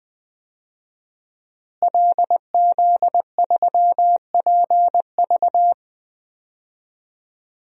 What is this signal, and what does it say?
Morse code, heard as a keyed tone, about 20 wpm, 700 Hz. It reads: LZ3PV
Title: DXCC-100 Call Signs - 5 Characters - Repeated 2x - 20wpm